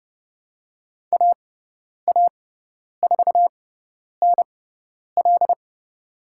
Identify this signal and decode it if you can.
Morse code, heard as a keyed tone, about 30 wpm, spaced as for 10 wpm, 700 Hz. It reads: AA4NL
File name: Lesson 16g - Letter c - Call Signs 30wpm